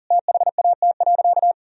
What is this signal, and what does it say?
Morse code, heard as a keyed tone, about 40 wpm, 700 Hz. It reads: THAT.
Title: Sentences from Top 1000 Words - Review Part 1 - 40wpm